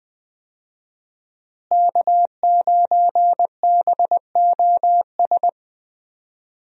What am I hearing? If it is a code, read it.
Morse code, heard as a keyed tone, about 21 wpm, 700 Hz. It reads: K9BOS
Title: USA Call Signs - 5 Characters - Repeated 3x - 20wpm